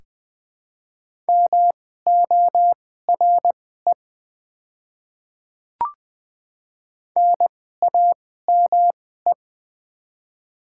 Morse code, heard as a keyed tone, about 20 wpm, spaced as for 15 wpm, 700 Hz. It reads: MORE NAME
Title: 2x Character Spacing - Top 500 Words - 20wpm